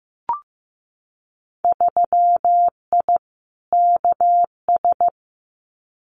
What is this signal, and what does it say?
Morse code, heard as a keyed tone, about 15 wpm, 700 Hz. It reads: 3I KS
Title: Winter Field Day Exchanges 15wpm